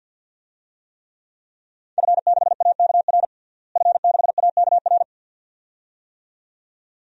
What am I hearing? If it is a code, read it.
Morse code, heard as a keyed tone, about 52 wpm, 700 Hz. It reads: F6AKR F6AKR
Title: DXCC-100 Call Signs - 5 Characters - Repeated 2x - 50wpm